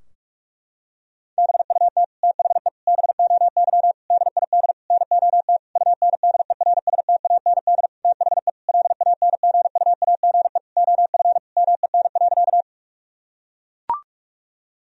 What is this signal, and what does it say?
Morse code, heard as a keyed tone, about 41 wpm, 700 Hz. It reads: BUT THE BOY DID NOT UNDERSTAND THE LANGUAGE OF MEN.
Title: Sentences from Top 1000 Words - Review Part 3 - 45wpm